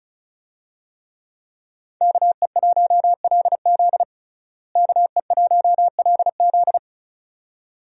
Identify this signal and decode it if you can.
Morse code, heard as a keyed tone, about 36 wpm, 700 Hz. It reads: KE1LZ KE1LZ
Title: USA Call Signs - 5 Characters - Repeated 2x - 35wpm